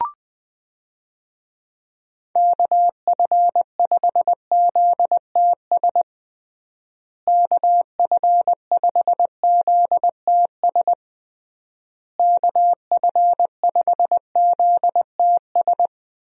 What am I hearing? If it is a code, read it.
Morse code, heard as a keyed tone, about 20 wpm, 700 Hz. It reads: KF5ZTS KF5ZTS KF5ZTS
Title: USA Call Signs - 6 Characters - Repeated 3x - 20wpm